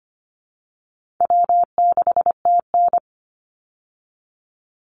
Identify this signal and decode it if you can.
Morse code, heard as a keyed tone, about 25 wpm, 700 Hz. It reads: W6TN
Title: Easy Call Signs Repeated 2x - 25wpm